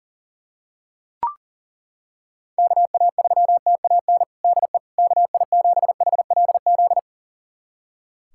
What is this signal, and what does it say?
Morse code, heard as a keyed tone, about 40 wpm, 700 Hz. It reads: KA3TAN DE KI7HLZ
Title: USA Callsign DE Callsign - 6 Characters - 40wpm